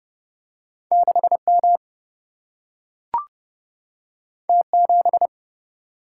Morse code, heard as a keyed tone, about 29 wpm, 700 Hz. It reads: 6M T7
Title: Single-Letter Single-Number 30wpm